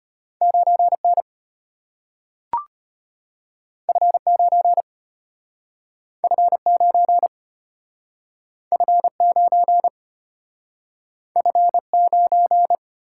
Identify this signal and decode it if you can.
Morse code, heard as a keyed tone, about 33 wpm, 700 Hz. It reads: 9N F9 F9 F9 F9